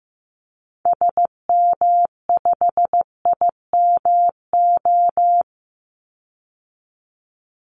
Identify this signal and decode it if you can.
Morse code, heard as a keyed tone, about 15 wpm, 700 Hz. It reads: SM5IMO
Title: Call Signs Repeated 3x - 15wpm